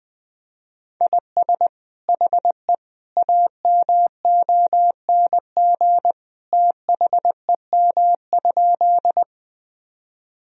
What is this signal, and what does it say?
Morse code, heard as a keyed tone, about 20 wpm, 700 Hz. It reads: IS HE AMONG THEM?
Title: Sentences from Top 1000 Words - Review Part 3 - 20wpm